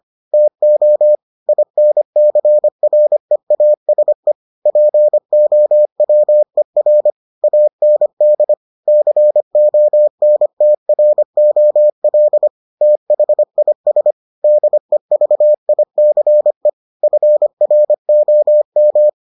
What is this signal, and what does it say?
Morse code, heard as a keyed tone, about 25 wpm, 600 Hz. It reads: TO INCREASE POWER AND CONTROL THIS DEVICE FROM